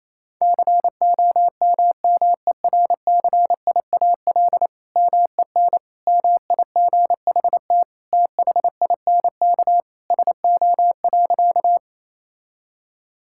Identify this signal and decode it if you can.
Morse code, heard as a keyed tone, about 28 wpm, 700 Hz. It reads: COMMERCIAL MEN MIGHT THINK SO.
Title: Sentences from Top 1000 Words - Review Part 5 - 28wpm